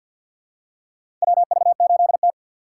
Morse code, heard as a keyed tone, about 52 wpm, 700 Hz. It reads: WV8T
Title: USA Call Signs - 4 Characters - 6x Character Spacing - 50wpm